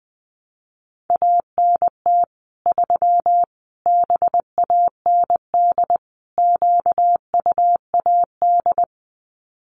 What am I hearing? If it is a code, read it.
Morse code, heard as a keyed tone, about 20 wpm, 700 Hz. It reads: ANT 3 BAND QUAD
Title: Common QSO Phrases 20wpm